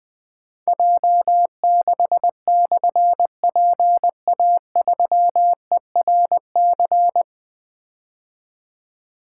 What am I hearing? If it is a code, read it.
Morse code, heard as a keyed tone, about 20 wpm, 700 Hz. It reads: J6/PA3ERC
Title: Difficult Call Signs 20wpm